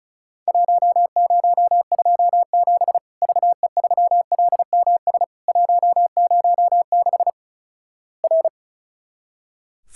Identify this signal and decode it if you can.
Morse code, heard as a keyed tone, about 34 wpm, 700 Hz. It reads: 1027 VE3LMS 106 R
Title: Worked All Europe DX Contest - QTC Exchange - 35wpm